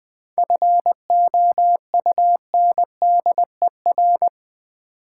Morse code, heard as a keyed tone, about 20 wpm, 700 Hz. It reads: FOUNDER
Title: 7 Letter Words - First 3 Letters Given - 20wpm